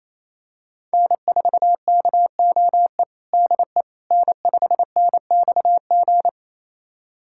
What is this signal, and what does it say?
Morse code, heard as a keyed tone, about 27 wpm, 700 Hz. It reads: N4KOE DE N5NXG
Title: USA Callsign DE Callsign - 5 Characters - 28wpm